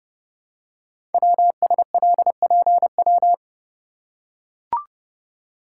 Morse code, heard as a keyed tone, about 28 wpm, 690 Hz. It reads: WSLPW
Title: Code Groups - 5 Letters - 30wpm